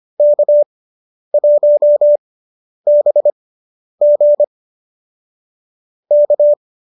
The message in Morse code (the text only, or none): K1BG K